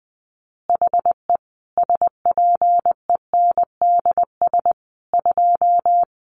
Morse code, heard as a keyed tone, about 20 wpm, 700 Hz. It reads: HE SPENDS 2
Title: Additional Easy Phrases 20wpm